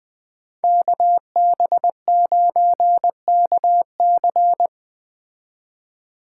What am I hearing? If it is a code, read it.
Morse code, heard as a keyed tone, about 19 wpm, 700 Hz. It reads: KB9KC